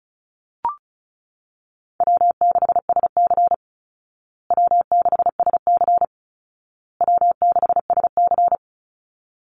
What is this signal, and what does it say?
Morse code, heard as a keyed tone, about 34 wpm, 700 Hz. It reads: W6SC W6SC W6SC
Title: Call Signs Repeated 2x - 35wpm